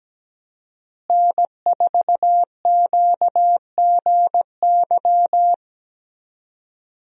Morse code, heard as a keyed tone, about 17 wpm, 700 Hz. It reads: N4QGY
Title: USA Call Signs - 5 Characters - Repeated 2x - 17wpm